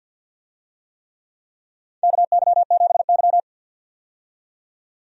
Morse code, heard as a keyed tone, about 53 wpm, 700 Hz. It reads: KY7Y
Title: USA Call Signs - 4 Characters - 50wpm